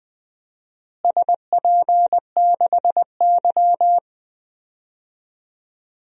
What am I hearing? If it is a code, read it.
Morse code, heard as a keyed tone, about 20 wpm, 700 Hz. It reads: SP6Y
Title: DXCC-100 Call Signs - 4 Characters - Repeated 2x - 20wpm